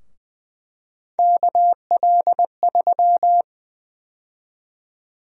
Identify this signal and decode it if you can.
Morse code, heard as a keyed tone, about 20 wpm, 700 Hz. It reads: KL3
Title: USA Call Sign Prefixes - 20wpm